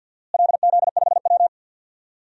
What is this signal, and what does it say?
Morse code, heard as a keyed tone, about 50 wpm, 700 Hz. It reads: LZ5W
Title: DXCC-100 Call Signs - 4 Characters - Repeated 2x - 50wpm